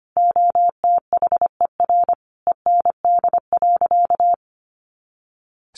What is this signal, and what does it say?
Morse code, heard as a keyed tone, about 25 wpm, 700 Hz. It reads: OTHER END.
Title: Easy Phrases 25wpm